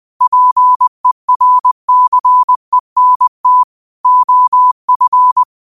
Morse code, heard as a keyed tone, about 20 wpm, 1000 Hz. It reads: PERCENT OF
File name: qotdfiles / 20wpm